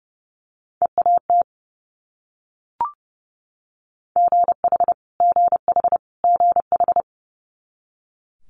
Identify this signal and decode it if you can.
Morse code, heard as a keyed tone, about 30 wpm, 700 Hz. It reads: EAT GH GH GH
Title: Multi-letter Phonemes Repeated 3x 30wpm